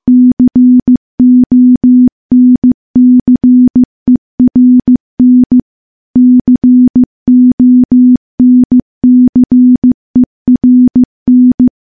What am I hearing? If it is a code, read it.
Morse code, heard as a keyed tone, about 15 wpm, 250 Hz. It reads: CONCERN CONCERN
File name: morse-training-15WPM